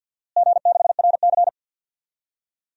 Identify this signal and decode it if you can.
Morse code, heard as a keyed tone, about 52 wpm, 700 Hz. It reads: G6RC
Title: DXCC-100 Call Signs - 4 Characters - Repeated 2x - 50wpm